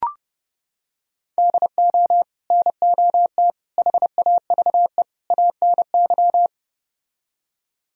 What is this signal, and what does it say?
Morse code, heard as a keyed tone, about 30 wpm, 700 Hz. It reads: DO NOT HAVE ANY